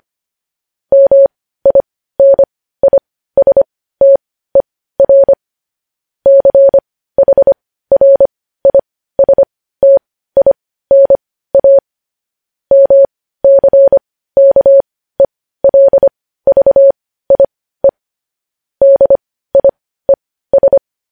Morse code, heard as a keyed tone, about 25 wpm, spaced as for 15 wpm, 550 Hz. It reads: MINISTER CHRISTINA MCKELVIE DIES
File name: podcast / News headlines-1x-25x15